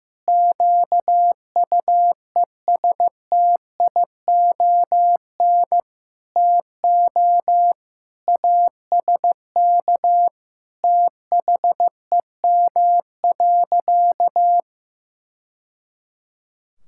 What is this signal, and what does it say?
Morse code, heard as a keyed tone, about 15 wpm, 700 Hz. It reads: QUESTION TO ASK THEM.
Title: Sentences from Top 1000 Words - Review Part 6 - 15wpm